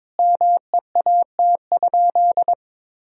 Morse code, heard as a keyed tone, about 22 wpm, 700 Hz. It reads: MEAT?
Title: Sentences from Top 1000 Words - Review Part 3 - 22wpm